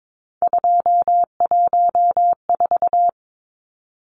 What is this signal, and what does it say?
Morse code, heard as a keyed tone, about 22 wpm, 700 Hz. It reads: 214
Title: Numbers - 3-Digit 22wpm